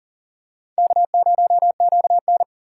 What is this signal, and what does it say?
Morse code, heard as a keyed tone, about 40 wpm, 700 Hz. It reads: K0QN
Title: USA Call Signs - 4 Characters - 2x Character Spacing - 40wpm